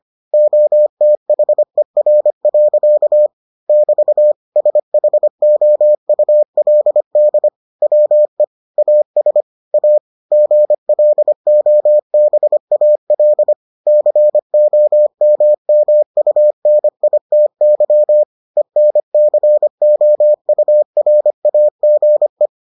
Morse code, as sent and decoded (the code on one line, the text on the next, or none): --- - .... .-. .-.-.- / -...- / ... .... --- ..- .-.. -.. / .-- . / .- ... / .- / --. .-.. --- -... .- .-.. / -.-. --- -- -- ..- -. .. - -.-- / . -. -.-. --- ..- .-. .- --. .
OTHER. <BT> SHOULD WE AS A GLOBAL COMMUNITY ENCOURAGE